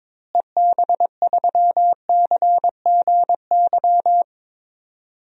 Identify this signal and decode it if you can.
Morse code, heard as a keyed tone, about 22 wpm, 700 Hz. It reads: EB3CGY
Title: DXCC-100 Call Signs - 6 Characters - Repeated 3x - 22wpm